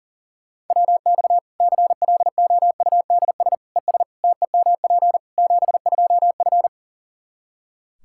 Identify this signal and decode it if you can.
Morse code, heard as a keyed tone, about 40 wpm, 700 Hz. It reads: WX CLOUDS ES TEMP 72F
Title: Practice QSO - Exchange 2 - Info - 40wpm